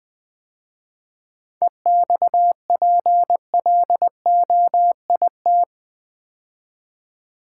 Morse code, heard as a keyed tone, about 20 wpm, 700 Hz. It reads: EXPLOIT
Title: 7 Letter Words - First 3 Letters Given - 20wpm